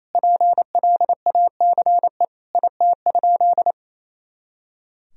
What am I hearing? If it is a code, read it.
Morse code, heard as a keyed tone, about 28 wpm, 700 Hz. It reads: PLACE IT?
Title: Sentences from Top 200 Words and Abbreviations 28wpm